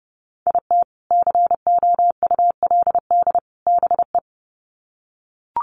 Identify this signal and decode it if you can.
Morse code, heard as a keyed tone, about 30 wpm, 700 Hz. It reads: IT COULD BE
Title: Sets of 3 Words 30wpm